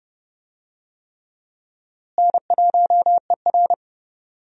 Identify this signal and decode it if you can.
Morse code, heard as a keyed tone, about 30 wpm, 700 Hz. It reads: N1ER
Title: USA Call Signs - 4 Characters - Repeated 3x - 30wpm